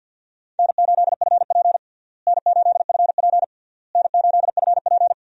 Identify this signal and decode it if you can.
Morse code, heard as a keyed tone, about 51 wpm, 700 Hz. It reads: N8FP N8FP N8FP
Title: USA Call Signs - 4 Characters - Repeated 3x - 50wpm